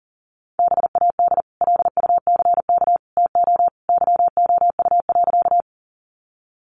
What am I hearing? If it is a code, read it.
Morse code, heard as a keyed tone, about 40 wpm, 700 Hz. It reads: BAD LUCK TO YOU.
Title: Sentences from Top 1500 Words 40wpm